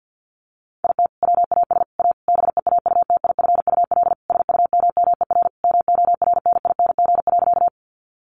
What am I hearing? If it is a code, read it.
Morse code, heard as a keyed tone, about 48 wpm, 700 Hz. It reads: IT WAS A BEAUTIFUL SUMMER MORNING.
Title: Sentences from Top 1100 Words 50wpm